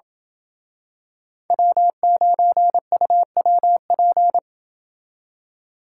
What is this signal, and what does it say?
Morse code, heard as a keyed tone, about 27 wpm, 700 Hz. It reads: W9UWP